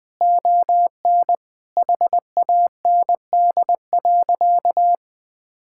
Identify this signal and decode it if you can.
Morse code, heard as a keyed tone, about 21 wpm, 700 Hz. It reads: ON HAND.
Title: Sentences from Top 1000 Words - Review Part 5 - 20wpm